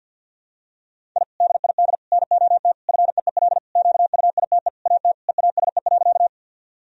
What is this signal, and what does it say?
Morse code, heard as a keyed tone, about 51 wpm, 700 Hz. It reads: I DID NOT FEEL QUITE AT EASE.
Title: Sentences from Top 1000 Words - Review Part 1 - 50wpm